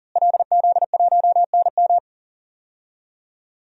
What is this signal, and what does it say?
Morse code, heard as a keyed tone, about 40 wpm, 700 Hz. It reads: LZ1NM
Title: DXCC-100 Call Signs - 5 Characters - Repeated 3x - 40wpm